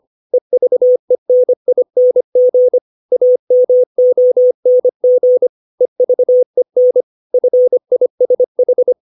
Morse code, heard as a keyed tone, about 25 wpm, 500 Hz. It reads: EVENING AMONG EVEN FISH
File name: 632 words / 632 Words 25WPM